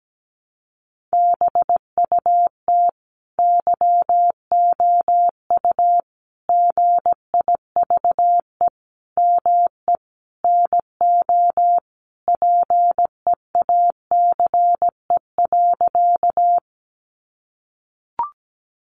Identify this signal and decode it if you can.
Morse code, heard as a keyed tone, about 17 wpm, 700 Hz. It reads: BUT YOU GIVE ME NO PEACE.
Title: Sentences from Top 1500 Words 17wpm